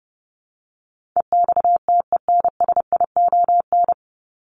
Morse code, heard as a keyed tone, about 28 wpm, 700 Hz. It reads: EXTENSION
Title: Top 1501-1750 Words 30wpm